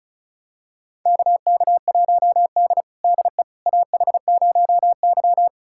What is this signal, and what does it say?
Morse code, heard as a keyed tone, about 34 wpm, 700 Hz. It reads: KK1D DE AH0Y
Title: USA Callsign DE Callsign - 4 Characters - 35wpm